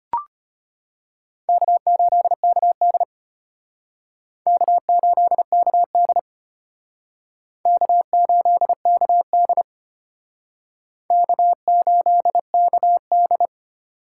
Morse code, heard as a keyed tone, about 32 wpm, 700 Hz. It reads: K8KD K8KD K8KD K8KD